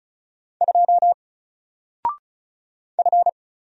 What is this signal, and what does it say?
Morse code, heard as a keyed tone, about 34 wpm, 700 Hz. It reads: 2 F
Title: Single Letter-Number Alt 35wpm